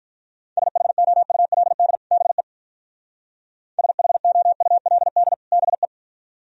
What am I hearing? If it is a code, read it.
Morse code, heard as a keyed tone, about 42 wpm, 700 Hz. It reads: SHOULD BE SHOULD BE
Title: Speed-Racing - Sets of 2 Words 35wpm